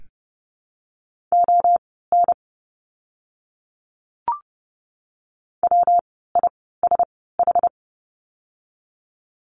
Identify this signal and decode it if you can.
Morse code, heard as a keyed tone, about 30 wpm, spaced as for 17 wpm, 700 Hz. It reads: ON WISH
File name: course / 30:10wpm / 115 Lesson 19f - Letter w - Words 30wpm